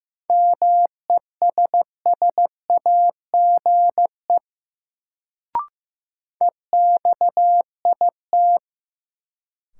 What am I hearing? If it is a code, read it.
Morse code, heard as a keyed tone, about 15 wpm, 700 Hz. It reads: MESSAGE EXIT
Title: Sentences from Top 1100 Words - New Word Drill - 15wpm